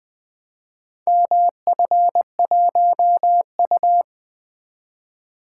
Morse code, heard as a keyed tone, about 24 wpm, 700 Hz. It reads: MF1U
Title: Speed-Racing - Four Letter-Number Combinations - 20wpm